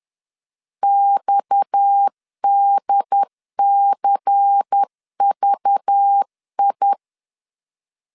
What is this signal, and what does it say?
Morse code, heard as a keyed tone, about 10 wpm, 800 Hz. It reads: XDCVI
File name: CW / DET / 1 6 1500C S C 10wpm 4252